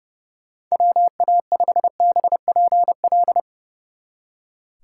Morse code, heard as a keyed tone, about 36 wpm, 700 Hz. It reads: WA5BPL
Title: Speed Racing - USA Call Signs 30wpm